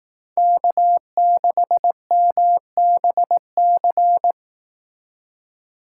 Morse code, heard as a keyed tone, about 18 wpm, 700 Hz. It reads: K6MBC
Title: Speed Racing - USA 1x3 Call Signs 15wpm